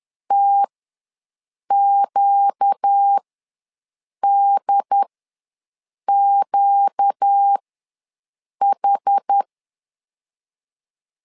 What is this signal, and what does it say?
Morse code, heard as a keyed tone, about 10 wpm, spaced as for 6 wpm, 800 Hz. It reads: TQDQH
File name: LEZIONI / 4 Q ws 10wpm 2306